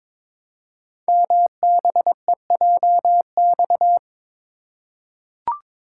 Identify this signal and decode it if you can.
Morse code, heard as a keyed tone, about 23 wpm, 700 Hz. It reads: MBEJX
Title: Code Groups - 5 Letters - 22wpm